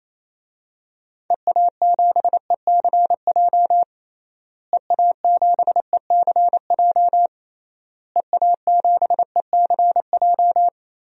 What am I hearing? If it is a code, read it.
Morse code, heard as a keyed tone, about 28 wpm, 700 Hz. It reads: EA7ECJ EA7ECJ EA7ECJ
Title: DXCC-100 Call Signs - 6 Characters - Repeated 3x - 28wpm